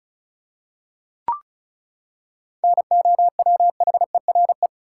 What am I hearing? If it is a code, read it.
Morse code, heard as a keyed tone, about 36 wpm, 700 Hz. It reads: NOWHERE
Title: Seven Letter Words - 35wpm